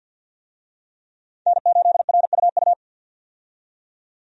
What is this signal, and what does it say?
Morse code, heard as a keyed tone, about 50 wpm, 700 Hz. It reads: N8RUU
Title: USA Call Signs - 5 Characters - 50wpm